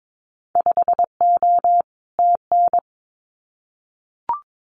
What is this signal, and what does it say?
Morse code, heard as a keyed tone, about 22 wpm, 700 Hz. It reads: 5O TN
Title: Winter Field Day Exchanges Encore 22wpm